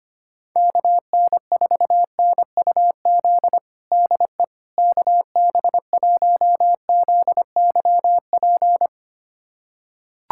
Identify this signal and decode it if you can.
Morse code, heard as a keyed tone, about 25 wpm, 700 Hz. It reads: KN4NUZ DE KB1ZYP
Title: USA Callsign DE Callsign 25wpm